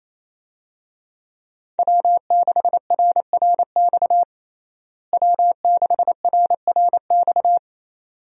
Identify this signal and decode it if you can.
Morse code, heard as a keyed tone, about 27 wpm, 700 Hz. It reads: W6RRX W6RRX
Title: USA Call Signs - 5 Characters - Repeated 2x - 28wpm